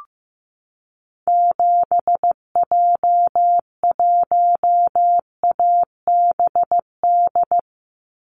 Morse code, heard as a keyed tone, about 15 wpm, 700 Hz. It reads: 7J1ABD
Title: Call Signs Repeated 3x - 15wpm